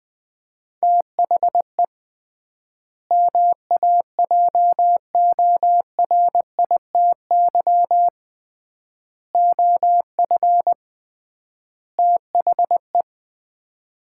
Morse code, heard as a keyed tone, about 20 wpm, 700 Hz. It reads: THE MAJORITY OF THE
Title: 3x Word-Spacing - Sets of 4 Words - 20wpm